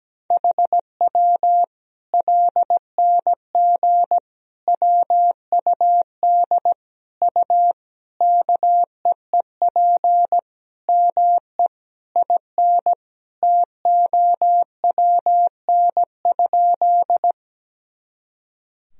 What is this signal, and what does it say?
Morse code, heard as a keyed tone, about 17 wpm, 700 Hz. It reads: HW LNG WUD U KEEP ME IN TOWN?
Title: Sentences from Top 500 Words and Abbreviations 17wpm